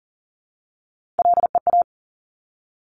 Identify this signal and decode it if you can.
Morse code, heard as a keyed tone, about 38 wpm, 700 Hz. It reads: LEA
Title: Top Three Letter Combinations in Words 40wpm